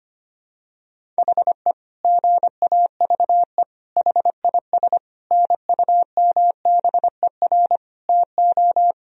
Morse code, heard as a keyed tone, about 25 wpm, 700 Hz. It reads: HE GAVE HIS NUMBER TO
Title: Sentences from Top 1000 Words - Review Part 1 - 25wpm